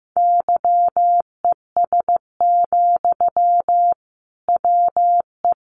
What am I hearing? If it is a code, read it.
Morse code, heard as a keyed tone, about 15 wpm, 700 Hz. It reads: YES, WE
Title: Sentences from Top 1100 Words 15wpm